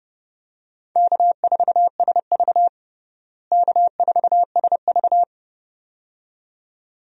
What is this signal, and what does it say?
Morse code, heard as a keyed tone, about 30 wpm, 700 Hz. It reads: K4SV K4SV
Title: USA Call Signs - 4 Characters - Repeated 2x - 30wpm